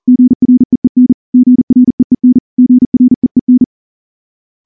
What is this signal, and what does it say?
Morse code, heard as a keyed tone, about 38 wpm, 250 Hz. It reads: GREEN GREEN GREEN
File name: morse-training-40WPM